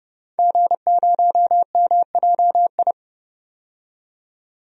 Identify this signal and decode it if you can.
Morse code, heard as a keyed tone, about 29 wpm, 700 Hz. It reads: G0MJI
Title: DXCC-100 Call Signs - 5 Characters - 30wpm